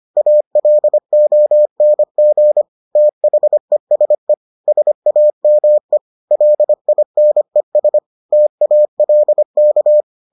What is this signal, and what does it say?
Morse code, heard as a keyed tone, about 25 wpm, 600 Hz. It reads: ALONG THESE SAME LINES TALK